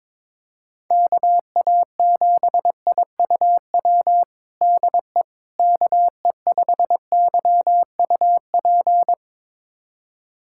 Morse code, heard as a keyed tone, about 22 wpm, 700 Hz. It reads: KA7IUW DE KE5YUP
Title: USA Callsign DE Callsign - 6 Characters - 22wpm